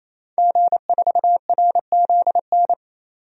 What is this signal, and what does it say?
Morse code, heard as a keyed tone, about 28 wpm, 700 Hz. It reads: G4RZN
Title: DXCC-100 Call Signs - 5 Characters - 28wpm